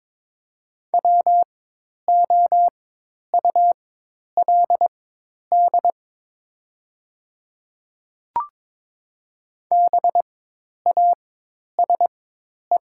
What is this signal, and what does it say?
Morse code, heard as a keyed tone, about 22 wpm, spaced as for 10 wpm, 700 Hz. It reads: WOULD BASE
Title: 4x Character Spacing - Top 500 Words - 22wpm